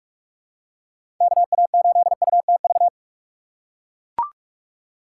Morse code, heard as a keyed tone, about 42 wpm, 700 Hz. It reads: KA8UTV
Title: USA Call Signs - 4x Character Spacing - 45wpm